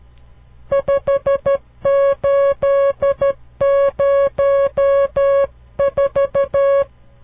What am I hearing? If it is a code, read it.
Morse code, heard as a keyed tone, about 12 wpm, 560 Hz. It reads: 5804
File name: CW Learning Programs / farnsworth / unit 5